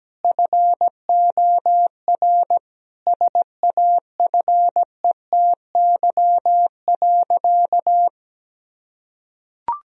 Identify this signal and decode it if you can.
Morse code, heard as a keyed tone, about 17 wpm, 700 Hz. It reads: FOR SAFETY.
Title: Sentences from Top 1000 Words - Review Part 4 - 17wpm